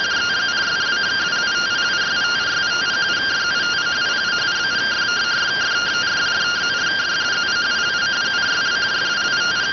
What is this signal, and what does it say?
Two-tone teleprinter signal, about 170 Hz shift.